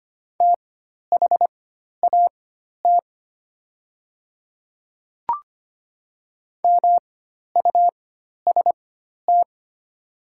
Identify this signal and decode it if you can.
Morse code, heard as a keyed tone, about 25 wpm, spaced as for 12 wpm, 700 Hz. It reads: THAT MUST
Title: 4x Character Spacing - Top 200 Words - 25wpm